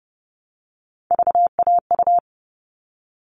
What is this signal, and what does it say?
Morse code, heard as a keyed tone, about 30 wpm, 700 Hz. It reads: VAU
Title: All Cut Numbers - 3-Digit 30wpm